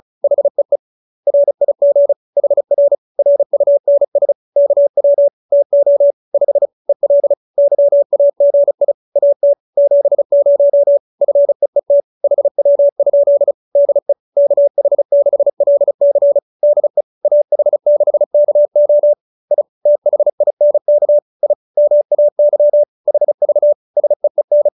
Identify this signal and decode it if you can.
Morse code, heard as a keyed tone, about 34 wpm, 570 Hz. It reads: HEE RIG HR RUNS KW TO 5 EL YAGI AT 70 FEET HW? DE KH6LC DE AH6KO I THINK I MAY HV SEEN